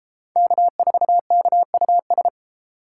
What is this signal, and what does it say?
Morse code, heard as a keyed tone, about 30 wpm, 700 Hz. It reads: K4KUS